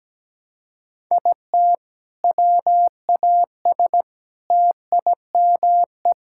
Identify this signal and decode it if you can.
Morse code, heard as a keyed tone, about 17 wpm, 700 Hz. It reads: IT WAS TIME
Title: Sentences from Top 1600 Words 17wpm